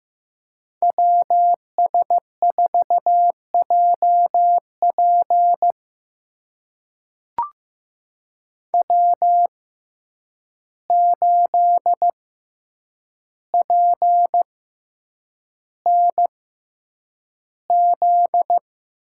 Morse code, heard as a keyed tone, about 15 wpm, spaced as for 8 wpm, 700 Hz. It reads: WS4JP W8PNZ
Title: USA Call Signs - 5 Characters - 6x Character Spacing - 15wpm